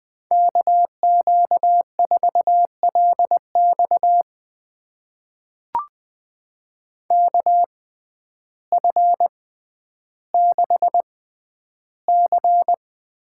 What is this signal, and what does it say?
Morse code, heard as a keyed tone, about 20 wpm, spaced as for 11 wpm, 700 Hz. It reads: KQ4LX KF6C
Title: USA Call Signs - 5 Characters - 6x Character Spacing - 20wpm